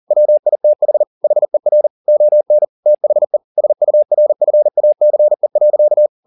Morse code, heard as a keyed tone, about 40 wpm, 600 Hz. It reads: WITH HER ON THE SURFACE.